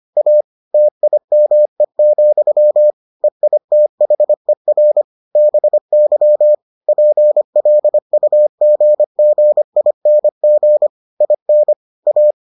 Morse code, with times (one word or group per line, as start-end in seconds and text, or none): 0.2-0.4 A
0.7-2.9 TIME,
3.2-5.0 EITHER
5.4-6.5 BY
6.9-10.9 PLUGGING
11.2-11.7 IN
12.1-12.3 A